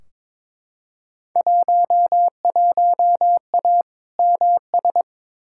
Morse code, heard as a keyed tone, about 22 wpm, 700 Hz. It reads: 11A MS
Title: Field Day Exchanges Encore - 22wpm